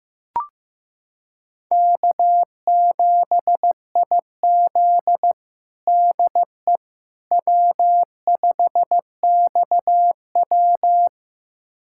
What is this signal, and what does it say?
Morse code, heard as a keyed tone, about 15 wpm, 700 Hz. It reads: K7IZ DE W5XW
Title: USA Callsign DE Callsign - 4 Characters - 15wpm